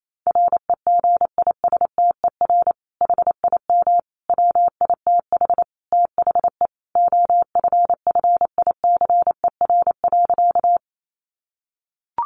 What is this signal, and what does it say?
Morse code, heard as a keyed tone, about 28 wpm, 700 Hz. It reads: REGISTER HIM WITH THE OFFICER.
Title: Sentences from Top 1200 Words 28wpm